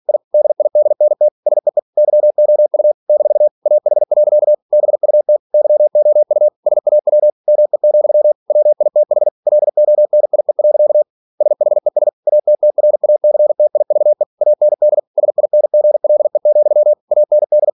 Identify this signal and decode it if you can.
Morse code, heard as a keyed tone, about 46 wpm, 600 Hz. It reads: I DIDNT SEE YOU - AH. BUT YOU SAW ME, WITH LONIE. SHES ATTRACTIVE AND SINGLE, AND
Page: Never 47wpm